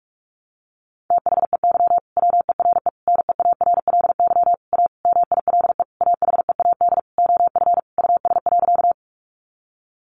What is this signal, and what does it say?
Morse code, heard as a keyed tone, about 44 wpm, 700 Hz. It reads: THEY WERE NEARLY A MILE AHEAD OF US.
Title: Sentences from Top 1300 Words 45wpm